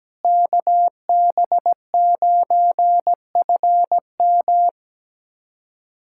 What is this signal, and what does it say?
Morse code, heard as a keyed tone, about 17 wpm, 700 Hz. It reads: KB9FM
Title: USA Callsign DE Callsign - 5 Characters - 17wpm